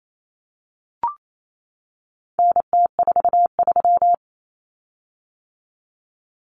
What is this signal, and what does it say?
Morse code, heard as a keyed tone, about 28 wpm, 700 Hz. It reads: NT43
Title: Common Cut Numbers - 4-Digit 28wpm